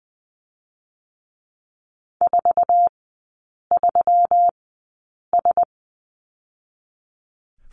Morse code, heard as a keyed tone, about 20 wpm, spaced as for 8 wpm, 700 Hz. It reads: 43S
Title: 121 Lesson 20e - Number 3 - Three Characters 20wpm